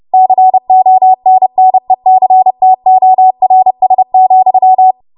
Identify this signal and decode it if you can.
Morse code, heard as a keyed tone, about 30 wpm, 750 Hz. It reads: CONNECTORS,
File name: Archive / 30 WPM / 30WPM